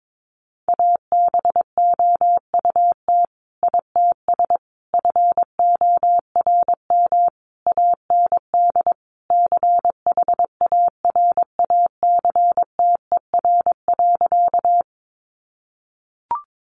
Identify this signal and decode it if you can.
Morse code, heard as a keyed tone, about 22 wpm, 700 Hz. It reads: ABOUT ITS FORM AND CHARACTER.
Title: Sentences from Top 1200 Words 22wpm